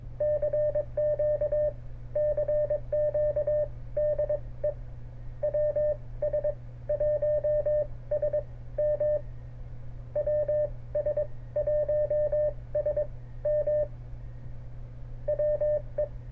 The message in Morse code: CQ CQ DE WS1SM WS1SM WE

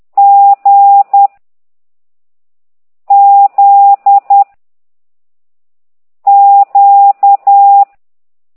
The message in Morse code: G Z Q

